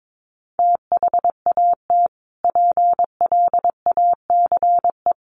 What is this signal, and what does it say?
Morse code, heard as a keyed tone, about 22 wpm, 700 Hz. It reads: THAT PLACE
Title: Sentences from Top 1500 Words 22wpm